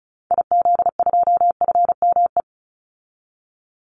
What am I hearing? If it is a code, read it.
Morse code, heard as a keyed tone, about 34 wpm, 700 Hz. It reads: IZ2FME
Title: Call Signs - 35wpm